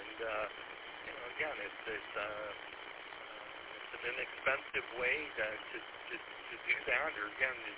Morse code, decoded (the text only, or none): none